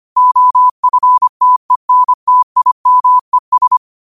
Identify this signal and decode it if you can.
Morse code, heard as a keyed tone, about 25 wpm, 1000 Hz. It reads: OFTENTIMES
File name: qotdfiles / 25wpm